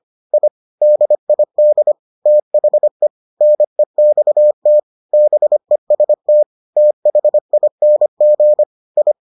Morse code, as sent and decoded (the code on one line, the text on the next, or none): .. / -.. .. -.. / - .... / -. -..- - / -... ... - / - .... .. -. --. / ..
I DID THE NEXT BEST THING I